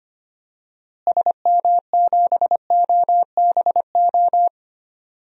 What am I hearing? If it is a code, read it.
Morse code, heard as a keyed tone, about 25 wpm, 700 Hz. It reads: SM7OBO
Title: DXCC-100 Call Signs - 6 Characters - Repeated 3x - 25wpm